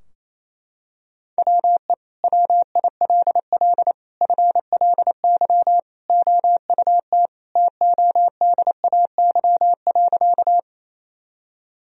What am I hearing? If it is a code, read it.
Morse code, heard as a keyed tone, about 28 wpm, 700 Hz. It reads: WE WILL FLY OUT TODAY.
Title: Sentences from Top 1000 Words - Review Part 2 - 28wpm